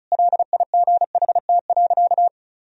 Morse code, heard as a keyed tone, about 36 wpm, 700 Hz. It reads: LIGHT.
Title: Sentences from Top 1000 Words - Review Part 1 - 35wpm